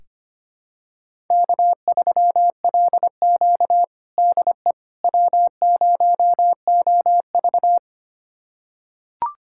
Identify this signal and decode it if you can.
Morse code, heard as a keyed tone, about 25 wpm, 700 Hz. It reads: K3LQ DE W0OV
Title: USA Callsign DE Callsign - 4 Characters - 25wpm